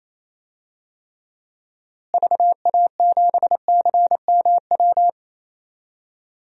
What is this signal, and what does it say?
Morse code, heard as a keyed tone, about 28 wpm, 700 Hz. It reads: VA7CMW
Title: DXCC-100 Call Signs - 6 Characters - Repeated 3x - 28wpm